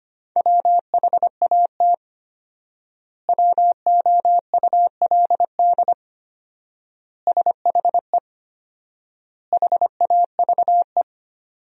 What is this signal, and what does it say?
Morse code, heard as a keyed tone, about 25 wpm, 700 Hz. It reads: WHAT WOULD SHE HAVE
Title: 4x Word-Spacing - Sentences from Top 100 Words - 25wpm